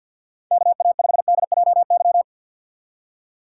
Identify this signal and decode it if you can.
Morse code, heard as a keyed tone, about 53 wpm, 700 Hz. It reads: KA5DJY
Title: USA Call Signs - 6 Characters - 4x Character Spacing - 50wpm